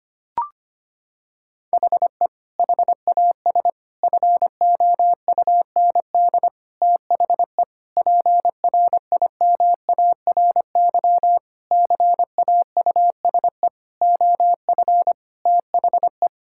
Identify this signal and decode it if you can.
Morse code, heard as a keyed tone, about 25 wpm, 700 Hz. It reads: HE HAS FOUND THE PRIMARY CAUSE OF THE
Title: Sentences from Top 1000 Words - Review Part 5 - 25wpm